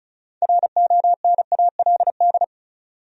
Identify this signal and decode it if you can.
Morse code, heard as a keyed tone, about 36 wpm, 700 Hz. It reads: RONALD
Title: USA Common Names - 35wpm